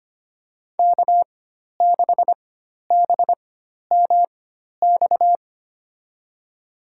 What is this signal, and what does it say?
Morse code, heard as a keyed tone, about 25 wpm, spaced as for 12 wpm, 700 Hz. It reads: K6BMX
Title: USA Call Signs - 4x Character Spacing - 25wpm